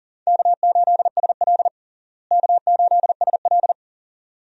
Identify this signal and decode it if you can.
Morse code, heard as a keyed tone, about 40 wpm, 700 Hz. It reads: K8SL K8SL
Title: USA Call Signs - 4 Characters - Repeated 2x - 40wpm